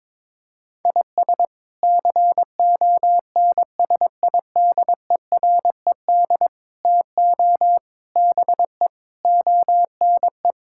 Morse code, heard as a keyed tone, about 22 wpm, 700 Hz. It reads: IS CONSIDERED TO BE ONE
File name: practice / 22wpm / Sets of 8 Words - 22wpm